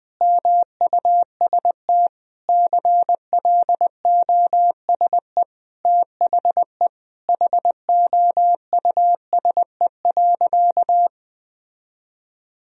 Morse code, heard as a keyed tone, about 20 wpm, 700 Hz. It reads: MUST CLOSE THE HOUSE.
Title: Sentences from Top 1000 Words - Review Part 1 - 20wpm